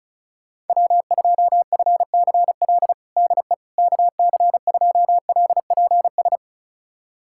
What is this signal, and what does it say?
Morse code, heard as a keyed tone, about 36 wpm, 700 Hz. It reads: W2FCL DE KC2LPS
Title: USA Callsign DE Callsign 35wpm